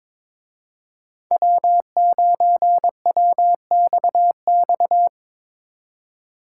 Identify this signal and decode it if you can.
Morse code, heard as a keyed tone, about 25 wpm, 700 Hz. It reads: W9WXX
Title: Speed Racing - USA 1x3 Call Signs 22wpm